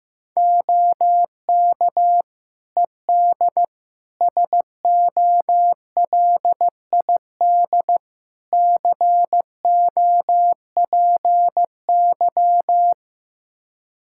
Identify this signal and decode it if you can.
Morse code, heard as a keyed tone, about 15 wpm, 700 Hz. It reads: OK ED SOLID COPY
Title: Practice QSO - Exchange 3 - More Info - 15wpm